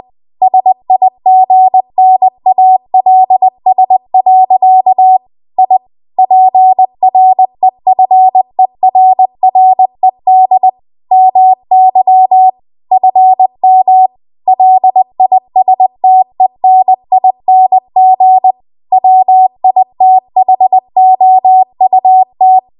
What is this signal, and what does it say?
Morse code, heard as a keyed tone, about 20 wpm, 750 Hz. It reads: SIGNALS. I PREFERRED MY FM LISTENING WITHOUT
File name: Archive / 20 WPM / 20WPM